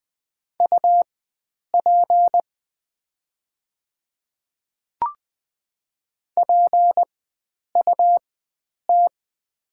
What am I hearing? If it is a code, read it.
Morse code, heard as a keyed tone, about 20 wpm, spaced as for 9 wpm, 700 Hz. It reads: UP PUT
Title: 4x Character Spacing - Top 300 Words - 20wpm